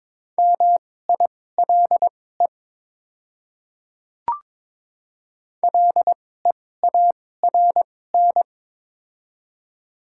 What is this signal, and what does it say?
Morse code, heard as a keyed tone, about 22 wpm, spaced as for 16 wpm, 700 Hz. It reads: MILE LEARN
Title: 2x Character Spacing - Top 300 Words - 22wpm